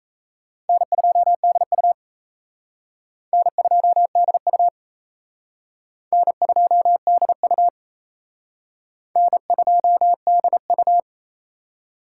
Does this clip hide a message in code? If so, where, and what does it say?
Yes: Morse code from 0.7 to 11.0 s, heard as a keyed tone, about 35 wpm, 700 Hz. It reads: N2DU N2DU N2DU N2DU